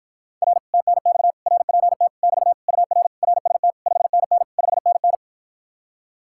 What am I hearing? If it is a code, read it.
Morse code, heard as a keyed tone, about 53 wpm, 700 Hz. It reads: R TNX RPT <BT> UR RST 5NN 5NN